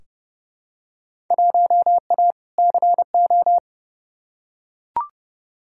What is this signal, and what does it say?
Morse code, heard as a keyed tone, about 31 wpm, 700 Hz. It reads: 1A CO
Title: Field Day Exchanges - 30wpm